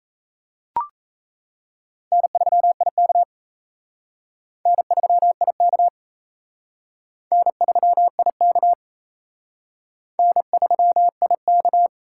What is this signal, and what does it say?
Morse code, heard as a keyed tone, about 36 wpm, 700 Hz. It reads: N3IK N3IK N3IK N3IK